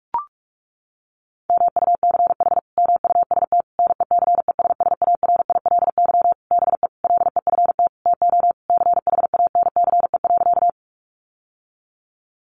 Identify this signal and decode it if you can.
Morse code, heard as a keyed tone, about 44 wpm, 700 Hz. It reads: MUCH MUST NECESSARILY BE LEFT TO CHANCE.
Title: Sentences from Top 1400 Words 45wpm